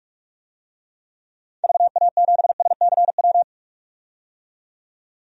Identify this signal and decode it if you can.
Morse code, heard as a keyed tone, about 43 wpm, 700 Hz. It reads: VA7SCW
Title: DXCC-100 Call Signs - 6 Characters - 45wpm